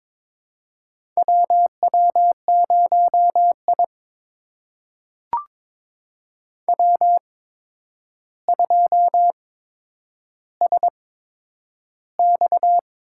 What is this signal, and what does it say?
Morse code, heard as a keyed tone, about 22 wpm, spaced as for 9 wpm, 700 Hz. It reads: WW0I W2SX
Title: USA Call Signs - 8x Character Spacing - 22wpm